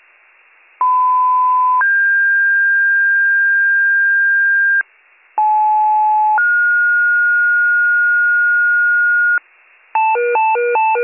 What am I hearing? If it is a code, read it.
Two-tone page, heard as a one-second tone, then a three-second tone, about 1010 then 1600 Hz, about 860 then 1430 Hz, with noise. It is heard from 0.8 to 9.4 s.